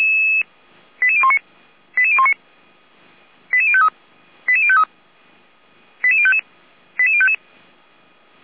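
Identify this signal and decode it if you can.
Five-tone selective call (ZVEI), heard as quick runs of five tones, tones of about 70 ms. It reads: 88019 88019 88053 88053 88055 88055
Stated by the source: Five-tone ZVEI selective call received on an SDR.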